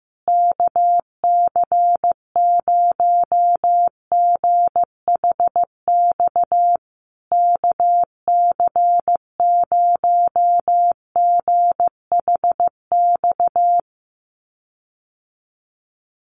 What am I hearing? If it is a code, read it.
Morse code, heard as a keyed tone, about 15 wpm, 700 Hz. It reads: KC0GHX KC0GHX
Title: USA Call Signs Repeated 3x 15wpm